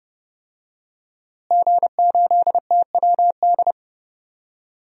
Morse code, heard as a keyed tone, about 32 wpm, 700 Hz. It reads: G8TWD
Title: DXCC-100 Call Signs - 5 Characters - Repeated 2x - 30wpm